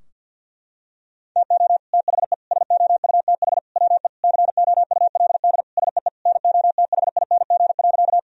Morse code, heard as a keyed tone, about 50 wpm, 700 Hz. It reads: TO THE SOUTH WE COULD SEE NOTHING.
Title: Sentences from Top 1000 Words - Review Part 2 - 50wpm